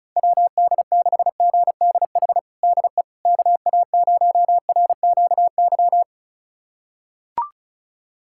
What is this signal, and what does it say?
Morse code, heard as a keyed tone, about 36 wpm, 700 Hz. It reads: WD6GDH DE KA0RQY
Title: USA Callsign DE Callsign - 6 Characters - 35wpm